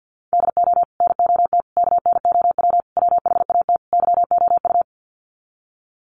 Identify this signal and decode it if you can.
Morse code, heard as a keyed tone, about 50 wpm, 700 Hz. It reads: DO NOT KNOW WHAT YOU
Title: Sets of 5 Words 50wpm